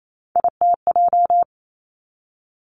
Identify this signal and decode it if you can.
Morse code, heard as a keyed tone, about 28 wpm, 700 Hz. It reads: ITJ
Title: Three Challenging Letter-Number Combinations 28wpm